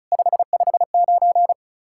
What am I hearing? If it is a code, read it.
Morse code, heard as a keyed tone, about 43 wpm, 700 Hz. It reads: 559
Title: Speed-Racing Common QSO Phrases - 35wpm